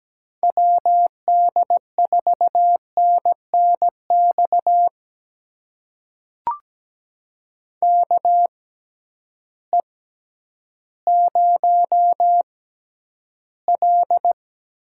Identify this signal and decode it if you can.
Morse code, heard as a keyed tone, about 17 wpm, spaced as for 10 wpm, 700 Hz. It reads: WD4NNX KE0L
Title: USA Call Signs - 6 Characters - 6x Character Spacing - 17wpm